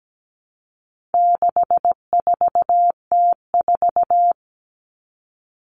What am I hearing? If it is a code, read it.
Morse code, heard as a keyed tone, about 17 wpm, 700 Hz. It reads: 64T4
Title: All Cut Numbers - 4-Digit 17wpm